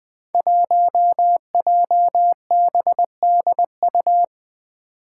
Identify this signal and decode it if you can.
Morse code, heard as a keyed tone, about 20 wpm, 700 Hz. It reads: 1JBDU
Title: Five Challenging Letter-Number Combinations 20wpm